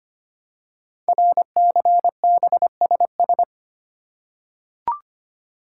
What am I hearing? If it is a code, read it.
Morse code, heard as a keyed tone, about 25 wpm, 700 Hz. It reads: RCBSS
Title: Code Groups - 5 Letters - 25wpm